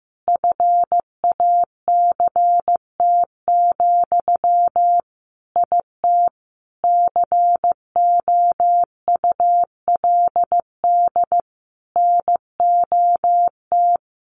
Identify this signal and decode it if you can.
Morse code, heard as a keyed tone, about 15 wpm, 700 Hz. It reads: FACT, IT COULD NOT
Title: Sentences from Top 1200 Words 15wpm